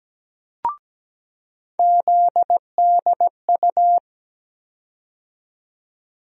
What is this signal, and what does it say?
Morse code, heard as a keyed tone, about 17 wpm, 700 Hz. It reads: ZDU
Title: USA Call Sign Postfixes - 3 Letters 17wpm